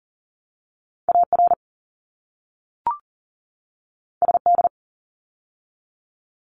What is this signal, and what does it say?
Morse code, heard as a keyed tone, about 42 wpm, 700 Hz. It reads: AR SD
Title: ARRL Sections 40wpm